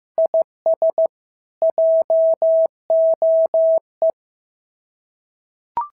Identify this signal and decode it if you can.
Morse code, heard as a keyed tone, about 14 wpm, 650 Hz. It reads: IS JOE
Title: SOTA QSO Practice 15wpm